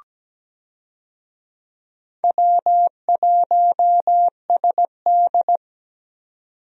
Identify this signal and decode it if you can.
Morse code, heard as a keyed tone, about 17 wpm, 700 Hz. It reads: W1SD
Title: USA Call Signs - 4 Characters - 17wpm